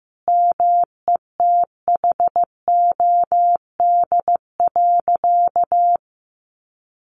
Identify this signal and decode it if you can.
Morse code, heard as a keyed tone, about 15 wpm, 700 Hz. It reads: METHOD.
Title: Sentences from Top 600 Words 15wpm